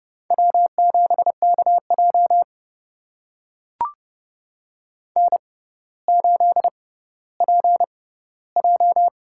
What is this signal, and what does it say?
Morse code, heard as a keyed tone, about 30 wpm, spaced as for 15 wpm, 700 Hz. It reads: W7KJ N8PJ
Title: USA Call Signs - 4 Characters - 6x Character Spacing - 30wpm